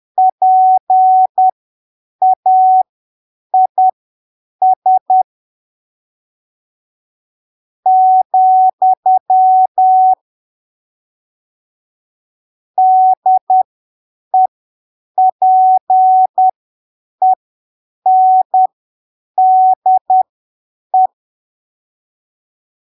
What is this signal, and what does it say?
Morse code, heard as a keyed tone, about 10 wpm, spaced as for 7 wpm, 750 Hz. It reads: PAIS , DEPENDE
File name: arquivos / texto / texto 22 10ppm